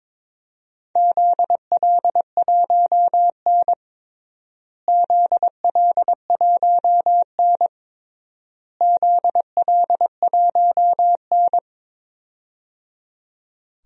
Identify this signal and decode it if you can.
Morse code, heard as a keyed tone, about 22 wpm, 700 Hz. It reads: ZL1N ZL1N ZL1N
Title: DXCC-100 Call Signs - 4 Characters - Repeated 3x - 22wpm